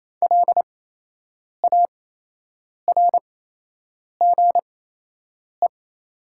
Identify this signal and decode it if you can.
Morse code, heard as a keyed tone, about 28 wpm, spaced as for 8 wpm, 700 Hz. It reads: LARGE
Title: 8x Character Spacing - Top 200 Words - 28wpm